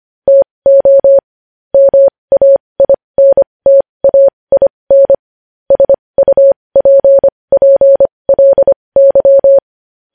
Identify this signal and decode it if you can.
Morse code, heard as a keyed tone, about 25 wpm, spaced as for 20 wpm, 550 Hz. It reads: TO MAINTAIN SUPPLY
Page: news-headlines-in-morse-code-at-20-wpm